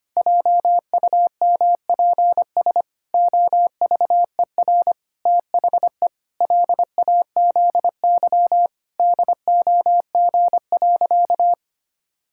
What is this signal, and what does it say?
Morse code, heard as a keyed tone, about 25 wpm, 700 Hz. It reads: JUMPS OVER THE LAZY DOG.
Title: Pangram Challenge v2 - 25wpm